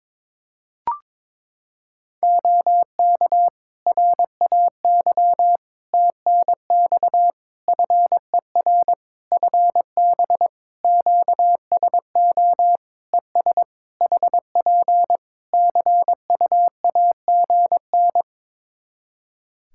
Morse code, heard as a keyed tone, about 22 wpm, 700 Hz. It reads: OK RAY TNX FER FB QSO ES HP CUAGN
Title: Practice QSO - Exchange 4 - Ending - 22wpm